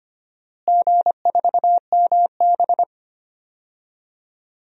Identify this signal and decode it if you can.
Morse code, heard as a keyed tone, about 25 wpm, 700 Hz. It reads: G4MB
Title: DXCC-100 Call Signs - 4 Characters - Repeated 3x - 25wpm